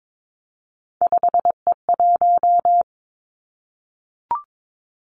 Morse code, heard as a keyed tone, about 22 wpm, 700 Hz. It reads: 5E1
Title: Three Challenging Letter-Number Combinations 22wpm